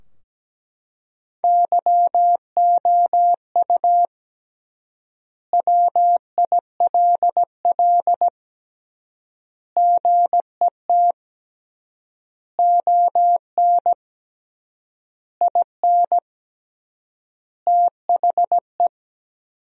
Morse code, heard as a keyed tone, about 17 wpm, 700 Hz. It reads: YOU WILL GET ON IN THE
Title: 3x Word-Spacing - Sentences from Top 200 Words - 17wpm